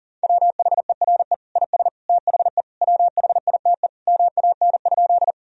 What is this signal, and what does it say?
Morse code, heard as a keyed tone, about 40 wpm, 700 Hz. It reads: WHERE IS THE WHITE MAN?
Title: Sentences from Top 1000 Words - Review Part 1 - 40wpm